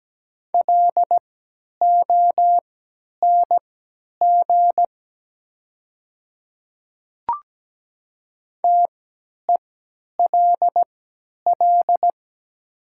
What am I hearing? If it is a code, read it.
Morse code, heard as a keyed tone, about 17 wpm, spaced as for 10 wpm, 700 Hz. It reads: LONG TELL
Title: 3x Character Spacing - Top 200 Words - 17wpm